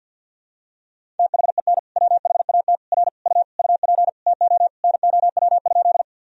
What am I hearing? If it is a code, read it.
Morse code, heard as a keyed tone, about 50 wpm, 700 Hz. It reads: THEN WHAT R U UP TO NOW?
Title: Sentences from Top 100 Words and Abbreviations 50wpm